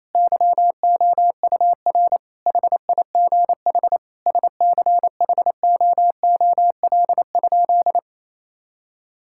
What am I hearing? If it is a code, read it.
Morse code, heard as a keyed tone, about 28 wpm, 700 Hz. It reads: YOUR HIGH SCHOOL?
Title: Sentences from Top 1000 Words - Review Part 6 - 28wpm